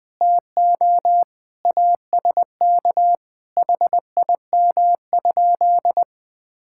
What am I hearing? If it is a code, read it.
Morse code, heard as a keyed tone, about 20 wpm, 700 Hz. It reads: TO ASK HIM?
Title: Sentences from Top 1600 Words 20wpm